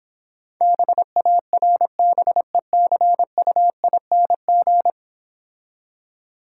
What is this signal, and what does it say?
Morse code, heard as a keyed tone, about 26 wpm, 700 Hz. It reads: BARBECUING